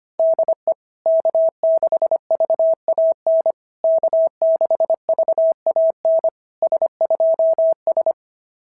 Morse code, heard as a keyed tone, about 24 wpm, 650 Hz. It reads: DE K6VAN K6VAN S2S
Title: SOTA QSO Practice 25wpm